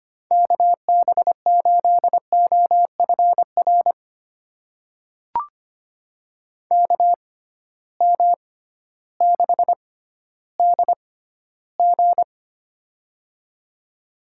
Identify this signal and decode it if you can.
Morse code, heard as a keyed tone, about 25 wpm, spaced as for 14 wpm, 700 Hz. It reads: KB8OFR KM6DG
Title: USA Call Signs - 6x Character Spacing - 25wpm